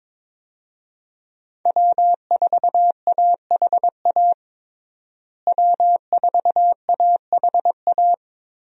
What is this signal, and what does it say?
Morse code, heard as a keyed tone, about 22 wpm, 700 Hz. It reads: W4AHA W4AHA
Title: USA Call Signs - 5 Characters - Repeated 3x - 22wpm